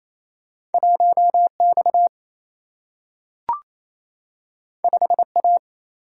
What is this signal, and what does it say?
Morse code, heard as a keyed tone, about 28 wpm, 700 Hz. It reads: 1X 5A
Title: Single-Letter Single-Number 28wpm